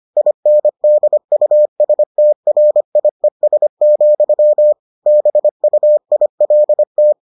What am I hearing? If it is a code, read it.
Morse code, heard as a keyed tone, about 25 wpm, 600 Hz. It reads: INDUSTRIES, BUILT